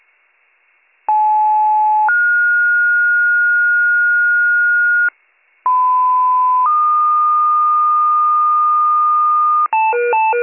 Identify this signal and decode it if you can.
Two-tone page, heard as a one-second tone, then a three-second tone, about 860 then 1430 Hz, about 1010 then 1220 Hz, with noise. It is heard from 1.1 to 9.7 s.